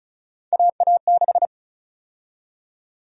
Morse code, heard as a keyed tone, about 34 wpm, 700 Hz. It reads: AA6
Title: USA Call Sign Prefixes - 35wpm